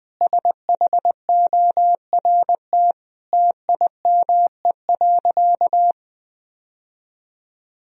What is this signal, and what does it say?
Morse code, heard as a keyed tone, about 20 wpm, 700 Hz. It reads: SHORT TIME.
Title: Sentences from Top 1000 Words - Review Part 2 - 20wpm